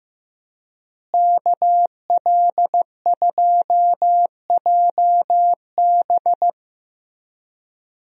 Morse code, heard as a keyed tone, about 15 wpm, 700 Hz. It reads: KL2JB
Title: USA Call Signs - 5 Characters - Repeated 3x - 15wpm